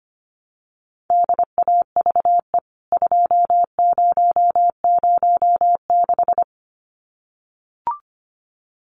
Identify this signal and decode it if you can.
Morse code, heard as a keyed tone, about 25 wpm, 700 Hz. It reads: DAVE 2006
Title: CWT Exchange 25wpm